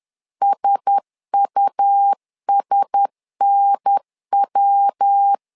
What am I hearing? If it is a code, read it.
Morse code, heard as a keyed tone, about 10 wpm, 800 Hz. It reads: SUSNW
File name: LEZIONI / 4 Q1S 10wpm 2133